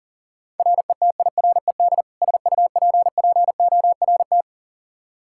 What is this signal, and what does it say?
Morse code, heard as a keyed tone, about 37 wpm, 700 Hz. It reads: RETIRED SUPPORT
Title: K1USN SST QSO Practice 40wpm